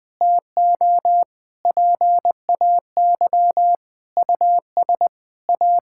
Morse code, heard as a keyed tone, about 20 wpm, 700 Hz. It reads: TO PAY US A
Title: Sentences from Top 1000 Words - Review Part 6 - 20wpm